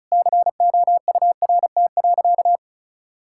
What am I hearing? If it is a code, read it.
Morse code, heard as a keyed tone, about 36 wpm, 700 Hz. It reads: COURT.
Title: Sentences from Top 1600 Words 35wpm